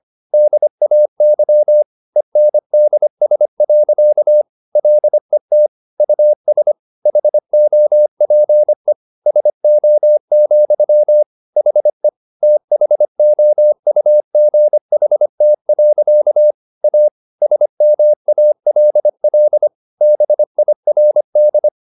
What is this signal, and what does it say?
Morse code, heard as a keyed tone, about 25 wpm, 600 Hz. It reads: DAY ENDS. LET US HOPE SO, HE THOUGHT. A SMALL BIRD
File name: the old man and the sea part 1 25wpm